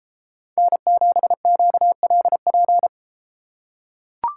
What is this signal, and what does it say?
Morse code, heard as a keyed tone, about 30 wpm, 700 Hz. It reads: N7QLP